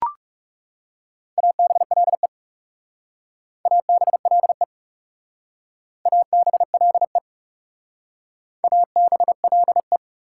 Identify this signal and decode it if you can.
Morse code, heard as a keyed tone, about 40 wpm, 700 Hz. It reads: ABLE ABLE ABLE ABLE